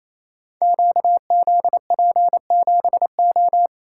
Morse code, heard as a keyed tone, about 28 wpm, 700 Hz. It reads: QZP7O
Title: Code Groups - 5 Letter:Numbers - 28wpm